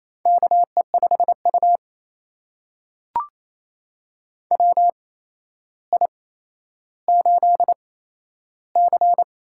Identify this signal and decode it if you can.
Morse code, heard as a keyed tone, about 28 wpm, spaced as for 12 wpm, 700 Hz. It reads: KE5U WI8C
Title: USA Call Signs - 4 Characters - 8x Character Spacing - 28wpm